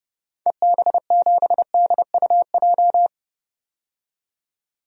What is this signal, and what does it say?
Morse code, heard as a keyed tone, about 30 wpm, 700 Hz. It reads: EB7DUJ
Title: DXCC-100 Call Signs - 6 Characters - 30wpm